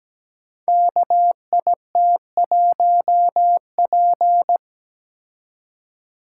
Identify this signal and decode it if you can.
Morse code, heard as a keyed tone, about 17 wpm, 700 Hz. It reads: KIT1P
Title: Code Groups - 5 Letter:Numbers - 17wpm